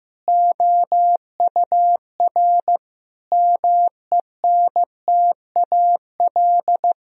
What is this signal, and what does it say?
Morse code, heard as a keyed tone, about 15 wpm, 700 Hz. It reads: OUR MENTAL
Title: Sentences from Top 1000 Words - Review Part 4 - 15wpm